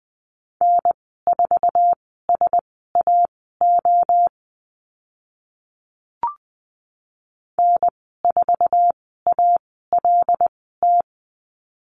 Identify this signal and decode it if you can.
Morse code, heard as a keyed tone, about 20 wpm, spaced as for 14 wpm, 700 Hz. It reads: N4SAO N4ALT
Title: 076 Lesson 13g - Letter L - Call Signs 20wpm